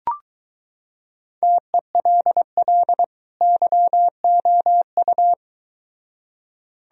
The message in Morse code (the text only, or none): TELL YOU